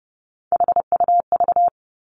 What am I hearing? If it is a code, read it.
Morse code, heard as a keyed tone, about 30 wpm, 700 Hz. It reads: HUV
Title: Three Challenging Letter-Number Combinations 30wpm